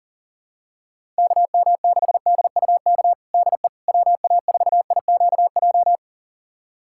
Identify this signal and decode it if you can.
Morse code, heard as a keyed tone, about 40 wpm, 700 Hz. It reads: KM6DUK DE WA4IQJ
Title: USA Callsign DE Callsign - 6 Characters - 40wpm